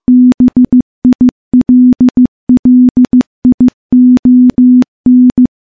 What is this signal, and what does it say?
Morse code, heard as a keyed tone, about 15 wpm, 250 Hz. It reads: BILLION
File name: morse / morse-training-15WPM